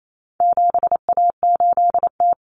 Morse code, heard as a keyed tone, about 28 wpm, 700 Hz. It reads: 7A8T
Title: Common Cut Numbers - 4-Digit 28wpm